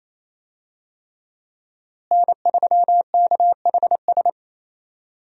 Morse code, heard as a keyed tone, about 28 wpm, 700 Hz. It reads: N3KHS
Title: USA Call Signs - 5 Characters - Repeated 3x - 28wpm